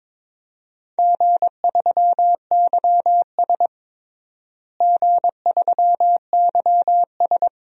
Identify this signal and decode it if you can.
Morse code, heard as a keyed tone, about 21 wpm, 700 Hz. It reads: G3YS G3YS
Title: DXCC-100 Call Signs - 4 Characters - Repeated 2x - 22wpm